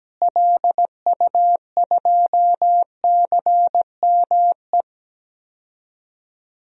Morse code, heard as a keyed tone, about 17 wpm, 700 Hz. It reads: LU2CME
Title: DXCC-100 Call Signs - 6 Characters - Repeated 3x - 17wpm